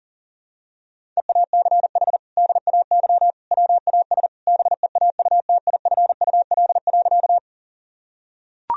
Morse code, heard as a keyed tone, about 40 wpm, 700 Hz. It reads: EACH DAY WAS BEAUTIFUL.
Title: Sentences from Top 1000 Words - Review Part 1 - 40wpm